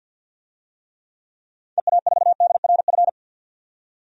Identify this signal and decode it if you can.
Morse code, heard as a keyed tone, about 52 wpm, 700 Hz. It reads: EA4DRF
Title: DXCC-100 Call Signs - 6 Characters - 50wpm